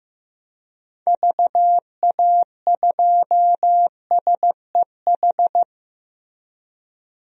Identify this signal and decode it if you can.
Morse code, heard as a keyed tone, about 15 wpm, 700 Hz. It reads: VA2SEH
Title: DXCC-100 Call Signs - 6 Characters - 15wpm